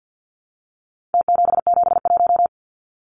Morse code, heard as a keyed tone, about 49 wpm, 700 Hz. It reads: T771
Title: Numbers - Contest Style - 4-Digit 50wpm